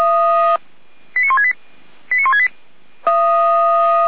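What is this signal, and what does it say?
Five-tone selective call (ZVEI), heard as quick runs of five tones, tones of about 80 ms. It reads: 89168 89168
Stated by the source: ZVEI five-tone call 89095 received with an SDR.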